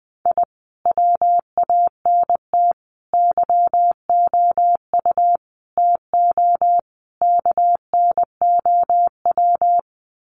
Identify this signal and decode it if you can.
Morse code, heard as a keyed tone, about 20 wpm, 700 Hz. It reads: I WANT YOU TO KNOW
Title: Sets of 5 Words Encore 20wpm